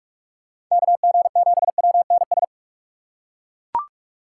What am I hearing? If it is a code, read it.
Morse code, heard as a keyed tone, about 41 wpm, 700 Hz. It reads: KG7WNS